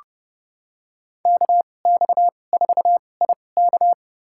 Morse code, heard as a keyed tone, about 30 wpm, spaced as for 22 wpm, 700 Hz. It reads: KX4IK
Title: USA Call Signs - 5 Characters - 2x Character Spacing - 30wpm